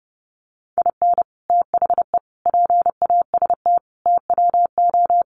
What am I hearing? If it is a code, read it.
Morse code, heard as a keyed tone, about 30 wpm, 700 Hz. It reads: IN THE PAST TWO
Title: Sets of 5 Words 30wpm